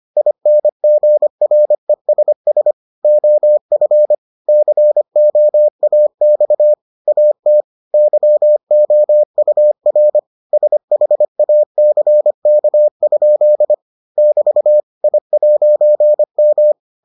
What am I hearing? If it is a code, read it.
Morse code, heard as a keyed tone, about 25 wpm, 600 Hz. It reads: INGRESS OF COAX AT YOUR SHACK? <BT> I'M